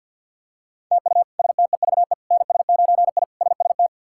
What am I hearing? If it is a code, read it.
Morse code, heard as a keyed tone, about 46 wpm, 700 Hz. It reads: TU STEVE NS9I SST
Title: K1USN SST QSO Practice - 50wpm